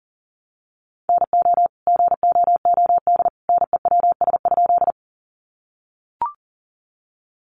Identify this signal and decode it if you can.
Morse code, heard as a keyed tone, about 40 wpm, 700 Hz. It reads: NO GOOD NEWS?
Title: Sentences from Top 800 Words 40wpm